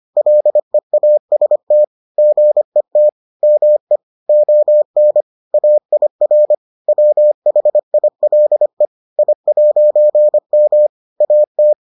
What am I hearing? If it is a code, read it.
Morse code, heard as a keyed tone, about 25 wpm, 600 Hz. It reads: LEAST GET ME ON AIR WHILE I'M AT